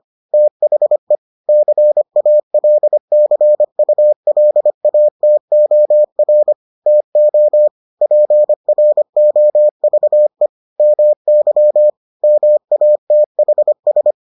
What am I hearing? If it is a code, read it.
Morse code, heard as a keyed tone, about 25 wpm, 600 Hz. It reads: THE CALCULATOR TO PROVE MY MATHS